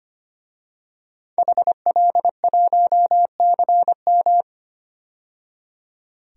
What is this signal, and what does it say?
Morse code, heard as a keyed tone, about 25 wpm, 700 Hz. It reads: HL1CM
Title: DXCC-100 Call Signs - 5 Characters - 25wpm